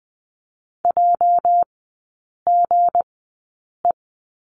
Morse code, heard as a keyed tone, about 20 wpm, spaced as for 8 wpm, 700 Hz. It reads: JGE
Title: 205 Lesson 33e - Letter j - Three Characters 20wpm